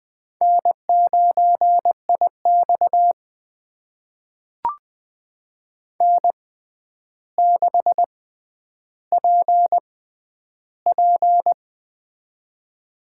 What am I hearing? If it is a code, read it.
Morse code, heard as a keyed tone, about 20 wpm, spaced as for 10 wpm, 700 Hz. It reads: N9IX N6PP
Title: USA Call Signs - 4 Characters - 6x Character Spacing - 20wpm